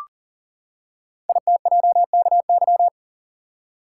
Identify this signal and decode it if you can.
Morse code, heard as a keyed tone, about 38 wpm, 690 Hz. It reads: ITJKY
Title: Code Groups - 5 Letters - 40wpm